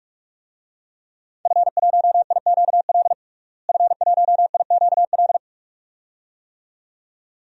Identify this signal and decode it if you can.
Morse code, heard as a keyed tone, about 44 wpm, 700 Hz. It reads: F1IQL F1IQL
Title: DXCC-100 Call Signs - 5 Characters - Repeated 2x - 45wpm